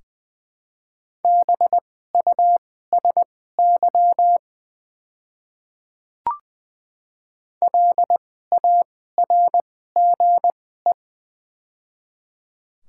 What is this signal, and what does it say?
Morse code, heard as a keyed tone, about 20 wpm, spaced as for 14 wpm, 700 Hz. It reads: BUSY LARGE
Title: 2x Character Spacing - Top 500 Words - 20wpm